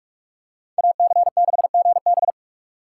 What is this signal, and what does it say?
Morse code, heard as a keyed tone, about 46 wpm, 700 Hz. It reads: AC6GB
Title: USA Call Signs - 5 Characters - 4x Character Spacing - 45wpm